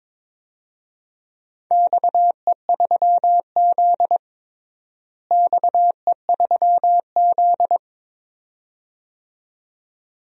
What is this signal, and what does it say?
Morse code, heard as a keyed tone, about 22 wpm, 700 Hz. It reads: XE3Z XE3Z
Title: DXCC-100 Call Signs - 4 Characters - Repeated 2x - 22wpm